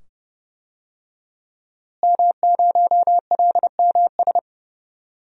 Morse code, heard as a keyed tone, about 30 wpm, 700 Hz. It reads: M0LMS
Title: DXCC-100 Call Signs - 5 Characters - Repeated 2x - 30wpm